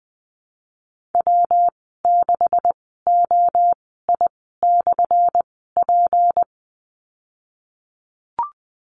Morse code, heard as a keyed tone, about 19 wpm, spaced as for 14 wpm, 700 Hz. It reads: W6OI/P
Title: 179 Lesson 29g - Slash - Call Signs 20wpm